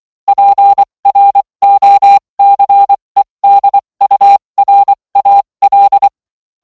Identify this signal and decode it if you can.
Morse code, heard as a keyed tone, about 24 wpm, spaced as for 20 wpm, 800 Hz. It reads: PROCEDURAL